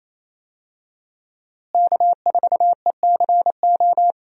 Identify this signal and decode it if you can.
Morse code, heard as a keyed tone, about 28 wpm, 700 Hz. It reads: K4ECO
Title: USA Call Signs - 5 Characters - Repeated 3x - 28wpm